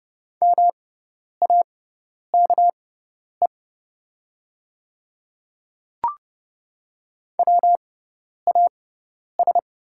Morse code, heard as a keyed tone, about 30 wpm, spaced as for 10 wpm, 700 Hz. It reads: MAKE WAS
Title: 6x Character Spacing - Top 100 Words - 30wpm